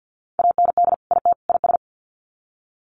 Morse code, heard as a keyed tone, about 50 wpm, 700 Hz. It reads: AND IT IS
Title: Sets of 3 Words 50wpm